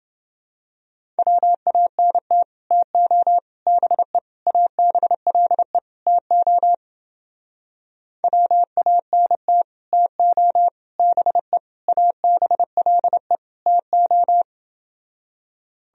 Morse code, heard as a keyed tone, about 30 wpm, 700 Hz. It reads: WANT TO BE ABLE TO WANT TO BE ABLE TO